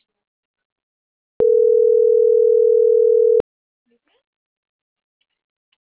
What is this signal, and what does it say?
Ringback tone (North America), 440 + 480 Hz.
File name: CENTRAL MAYAB / out atzhyry.gomez - .81990